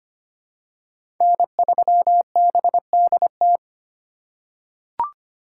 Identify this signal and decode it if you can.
Morse code, heard as a keyed tone, about 25 wpm, 700 Hz. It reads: N3BDT
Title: USA Call Signs - 5 Characters - 3x Character Spacing - 25wpm